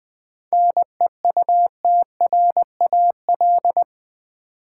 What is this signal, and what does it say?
Morse code, heard as a keyed tone, about 20 wpm, 700 Hz. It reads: NEUTRAL
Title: 7 Letter Words - First 3 Letters Given - 20wpm